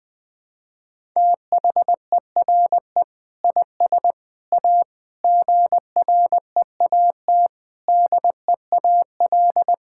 Morse code, heard as a keyed tone, about 20 wpm, 700 Hz. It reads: THERE IS A GREAT DEAL
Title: Sets of 6 Words - 20wpm